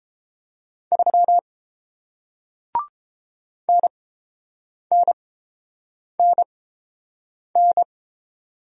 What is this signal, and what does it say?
Morse code, heard as a keyed tone, about 30 wpm, spaced as for 7 wpm, 700 Hz. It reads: 3 NNNN